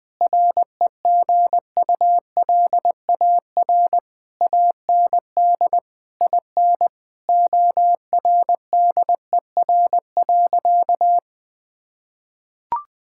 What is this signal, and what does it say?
Morse code, heard as a keyed tone, about 20 wpm, 700 Hz. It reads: REGULAR AND IN ORDER.
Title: Sentences from Top 1000 Words - Review Part 6 - 20wpm